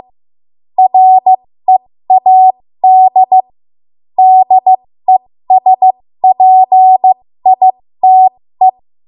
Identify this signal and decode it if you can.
Morse code, heard as a keyed tone, about 15 wpm, spaced as for 13 wpm, 750 Hz. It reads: READ DESPITE
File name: Morse / 13WPM